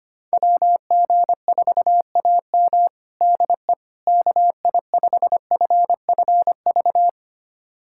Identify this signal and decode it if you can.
Morse code, heard as a keyed tone, about 25 wpm, 700 Hz. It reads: WG4AM DE KI5FFV
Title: USA Callsign DE Callsign 25wpm